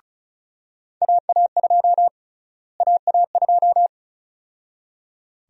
Morse code, heard as a keyed tone, about 34 wpm, 700 Hz. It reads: AA2 AA2
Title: USA Call Sign Prefixes - Repeated 2x - 35wpm